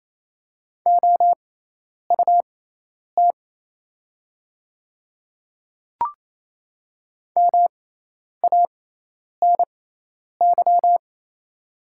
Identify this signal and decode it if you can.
Morse code, heard as a keyed tone, about 28 wpm, spaced as for 10 wpm, 690 Hz. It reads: OUT MANY
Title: 6x Character Spacing - Top 100 Words - 28wpm